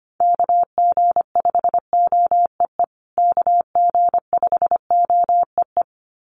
Morse code, heard as a keyed tone, about 25 wpm, 700 Hz. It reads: KG5OEE KG5OEE
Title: USA Call Signs Repeated 2x 25wpm